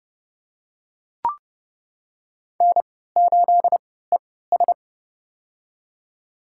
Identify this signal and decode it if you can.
Morse code, heard as a keyed tone, about 30 wpm, spaced as for 17 wpm, 700 Hz. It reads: N8ES
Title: USA Call Signs - 4 Characters - 3x Character Spacing - 30wpm